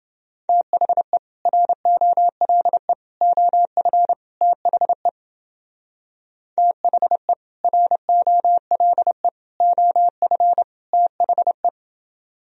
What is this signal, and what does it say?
Morse code, heard as a keyed tone, about 30 wpm, 700 Hz. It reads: THE ROLE OF THE THE ROLE OF THE